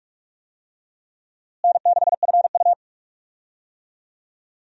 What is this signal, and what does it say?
Morse code, heard as a keyed tone, about 47 wpm, 700 Hz. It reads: N6FU
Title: USA Call Signs - 4 Characters - Repeated 3x - 45wpm